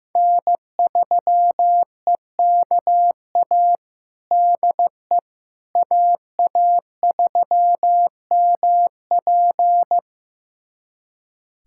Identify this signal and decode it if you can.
Morse code, heard as a keyed tone, about 15 wpm, 700 Hz. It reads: N3EKA DE AA3MP
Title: USA Callsign DE Callsign - 5 Characters - 15wpm